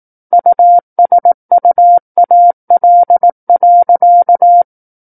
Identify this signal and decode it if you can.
Morse code, heard as a keyed tone, about 18 wpm, 700 Hz. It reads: USUAL.
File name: 18 WPM